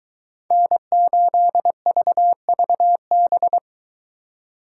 Morse code, heard as a keyed tone, about 23 wpm, 700 Hz. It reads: N8VVB